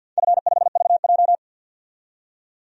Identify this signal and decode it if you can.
Morse code, heard as a keyed tone, about 48 wpm, 700 Hz. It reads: F5VJ
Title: DXCC-100 Call Signs - 4 Characters - 50wpm